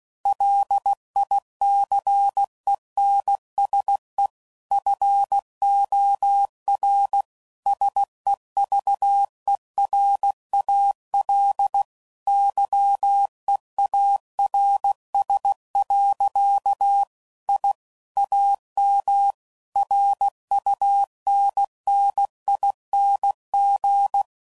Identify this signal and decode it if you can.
Morse code, heard as a keyed tone, about 16 wpm, 800 Hz. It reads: LICENSE FOR SEVERAL YEARS. I AM RUNNING
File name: CW Code Groups @15WPM / QSOs / WA7QOX QRX